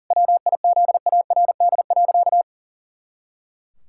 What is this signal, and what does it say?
Morse code, heard as a keyed tone, about 40 wpm, 700 Hz. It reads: WIZARD.
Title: Pangram Challenge v2 - 40wpm